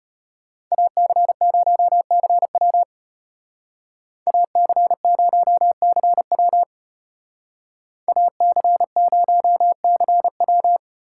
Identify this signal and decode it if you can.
Morse code, heard as a keyed tone, about 32 wpm, 700 Hz. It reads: AC0CW AC0CW AC0CW